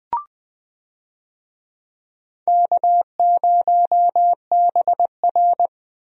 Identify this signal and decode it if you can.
Morse code, heard as a keyed tone, about 21 wpm, 700 Hz. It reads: K0BR
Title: USA Call Signs - 4 Characters - Repeated 2x - 20wpm